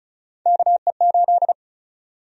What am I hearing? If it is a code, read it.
Morse code, heard as a keyed tone, about 36 wpm, 700 Hz. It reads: KE8
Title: USA Call Sign Prefixes - 35wpm